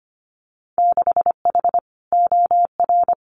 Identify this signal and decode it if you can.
Morse code, heard as a keyed tone, about 25 wpm, 700 Hz. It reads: 6H OR
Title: Winter Field Day Exchanges Encore 25wpm